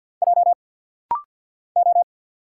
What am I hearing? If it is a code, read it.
Morse code, heard as a keyed tone, about 50 wpm, 700 Hz. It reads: J O
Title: Single Letters - Rapid-Fire 50wpm